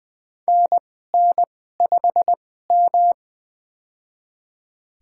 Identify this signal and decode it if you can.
Morse code, heard as a keyed tone, about 21 wpm, spaced as for 15 wpm, 700 Hz. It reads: NN5M
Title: USA Call Signs - 4 Characters - 2x Character Spacing - 20wpm